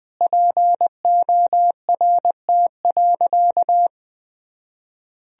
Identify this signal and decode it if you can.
Morse code, heard as a keyed tone, about 20 wpm, 700 Hz. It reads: PORT.
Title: Sentences from Top 1000 Words - Review Part 1 - 20wpm